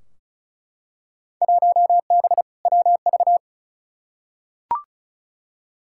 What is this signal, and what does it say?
Morse code, heard as a keyed tone, about 36 wpm, 700 Hz. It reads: 1B WV
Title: Field Day Exchanges - 35wpm